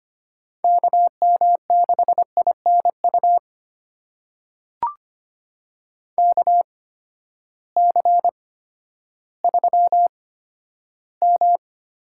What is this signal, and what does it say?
Morse code, heard as a keyed tone, about 25 wpm, spaced as for 12 wpm, 700 Hz. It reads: KM6INU KC3M
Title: USA Call Signs - 6 Characters - 8x Character Spacing - 25wpm